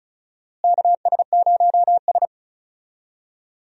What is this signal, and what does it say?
Morse code, heard as a keyed tone, about 34 wpm, 700 Hz. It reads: KS0S
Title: USA Call Signs - 4 Characters - Repeated 2x - 35wpm